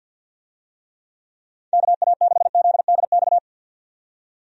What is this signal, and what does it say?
Morse code, heard as a keyed tone, about 48 wpm, 700 Hz. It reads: KA6ZDX
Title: USA Call Signs - 6 Characters - 50wpm